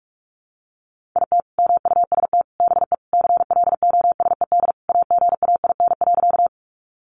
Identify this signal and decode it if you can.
Morse code, heard as a keyed tone, about 44 wpm, 700 Hz. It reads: IT MUST BE CLOSED AGAIN.
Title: Sentences from Top 1400 Words 45wpm